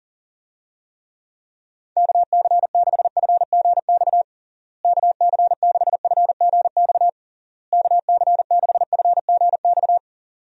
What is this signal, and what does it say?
Morse code, heard as a keyed tone, about 37 wpm, 700 Hz. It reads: KC6FGX KC6FGX KC6FGX
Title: USA Call Signs - 6 Characters - Repeated 3x - 40wpm